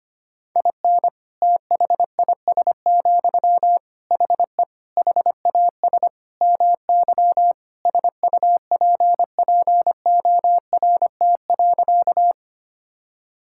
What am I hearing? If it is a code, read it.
Morse code, heard as a keyed tone, about 25 wpm, 700 Hz. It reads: IN THIS, HE HAS MY SUPPORT.
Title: Sentences from Top 1000 Words - Review Part 5 - 25wpm